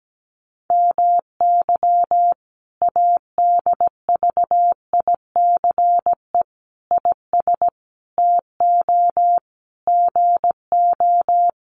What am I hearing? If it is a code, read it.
Morse code, heard as a keyed tone, about 17 wpm, 700 Hz. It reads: MY ADVICE IS TO GO
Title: Sentences from Top 1300 Words 17wpm